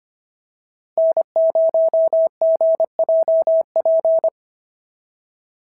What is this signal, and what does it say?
Morse code, heard as a keyed tone, about 25 wpm, 650 Hz. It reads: N0GJP
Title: SOTA QSO Practice 25wpm